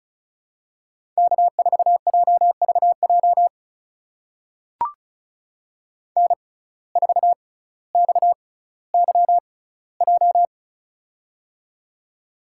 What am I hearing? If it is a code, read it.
Morse code, heard as a keyed tone, about 36 wpm, spaced as for 18 wpm, 700 Hz. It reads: K4JVJ N4XYJ
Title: USA Call Signs - 6x Character Spacing - 35wpm